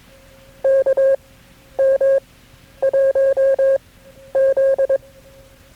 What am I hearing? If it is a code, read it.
Morse code, heard as a keyed tone, about 22 wpm, spaced as for 11 wpm, 540 Hz. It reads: KM1Z
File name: KB6MT-tape3-side1